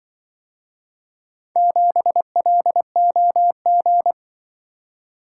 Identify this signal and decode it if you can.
Morse code, heard as a keyed tone, about 24 wpm, 700 Hz. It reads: 7LOG
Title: Speed-Racing - Four Letter-Number Combinations - 20wpm